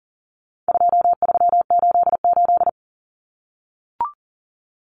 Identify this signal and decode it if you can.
Morse code, heard as a keyed tone, about 40 wpm, 700 Hz. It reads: 2388
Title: Numbers - 4-Digit 40wpm